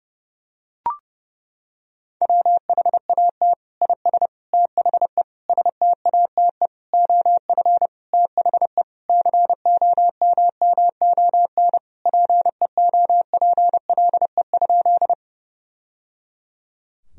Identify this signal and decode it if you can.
Morse code, heard as a keyed tone, about 30 wpm, 700 Hz. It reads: WHAT IS THE STATE OF THE COMMON PEOPLE?
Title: Sentences from Top 1000 Words - Review Part 3 - 30wpm